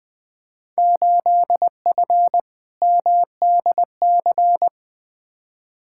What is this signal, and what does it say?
Morse code, heard as a keyed tone, about 20 wpm, 700 Hz. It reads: 8F MDC
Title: Field Day Exchanges Encore - 20wpm